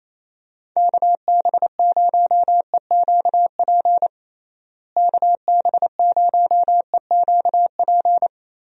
Morse code, heard as a keyed tone, about 28 wpm, 700 Hz. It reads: KB0EQP KB0EQP
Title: USA Call Signs - 6 Characters - Repeated 2x - 28wpm